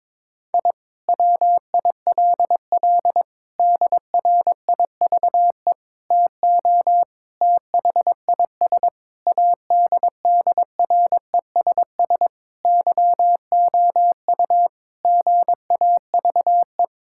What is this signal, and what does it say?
Morse code, heard as a keyed tone, about 22 wpm, 700 Hz. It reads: I WILL DRIVE TO THIS ADDRESS YOU GAVE
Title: Sentences from Top 1000 Words - Review Part 6 - 22wpm